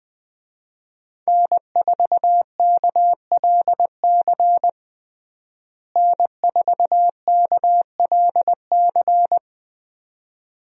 Morse code, heard as a keyed tone, about 21 wpm, 700 Hz. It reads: N4KLC N4KLC
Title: USA Call Signs - 5 Characters - Repeated 2x - 20wpm